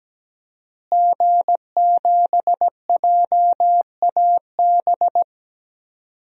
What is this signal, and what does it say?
Morse code, heard as a keyed tone, about 17 wpm, 700 Hz. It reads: G7JAB
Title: DXCC-100 Call Signs - 5 Characters - Repeated 2x - 17wpm